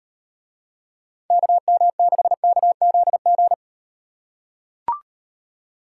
Morse code, heard as a keyed tone, about 37 wpm, 700 Hz. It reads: KM6KZG